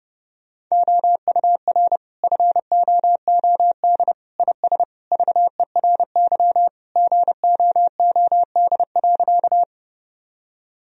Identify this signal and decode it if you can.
Morse code, heard as a keyed tone, about 30 wpm, 700 Hz. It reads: OUR FOOD IS VERY GOOD.
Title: Sentences from Top 1000 Words - Review Part 1 - 30wpm